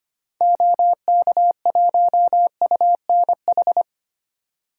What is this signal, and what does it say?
Morse code, heard as a keyed tone, about 25 wpm, 700 Hz. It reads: OK1UNH
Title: DXCC-100 Call Signs - 6 Characters - 25wpm